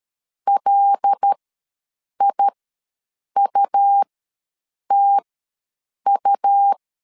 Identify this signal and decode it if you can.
Morse code, heard as a keyed tone, about 13 wpm, spaced as for 7 wpm, 800 Hz. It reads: LIUTU
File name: LEZIONI / 5 LBVC / 5 L ws 12wpm 2958